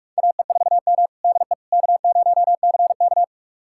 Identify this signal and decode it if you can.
Morse code, heard as a keyed tone, about 46 wpm, 700 Hz. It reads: AE4M DE K0CK
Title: USA Callsign DE Callsign - 4 Characters - 45wpm